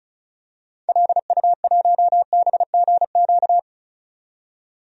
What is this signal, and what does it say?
Morse code, heard as a keyed tone, about 35 wpm, 700 Hz. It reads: LU1BGQ
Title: DXCC-100 Call Signs - 6 Characters - 35wpm